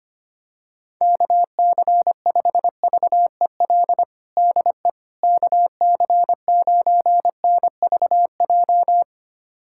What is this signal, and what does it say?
Morse code, heard as a keyed tone, about 25 wpm, 700 Hz. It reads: KC5VEL DE KC9NVJ
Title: USA Callsign DE Callsign - 6 Characters - 25wpm